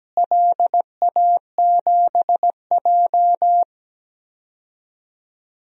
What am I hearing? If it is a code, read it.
Morse code, heard as a keyed tone, about 17 wpm, 700 Hz. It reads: LA7J
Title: DXCC-100 Call Signs - 4 Characters - Repeated 3x - 17wpm